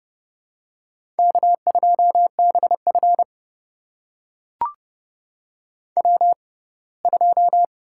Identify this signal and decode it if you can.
Morse code, heard as a keyed tone, about 30 wpm, spaced as for 20 wpm, 700 Hz. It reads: K2BF W2
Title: USA Call Signs - 4 Characters - 6x Character Spacing - 30wpm